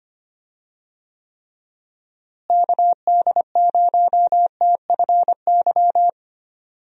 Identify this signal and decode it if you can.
Morse code, heard as a keyed tone, about 25 wpm, 700 Hz. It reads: KD0TFY
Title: USA Call Signs - 6 Characters - Repeated 2x - 25wpm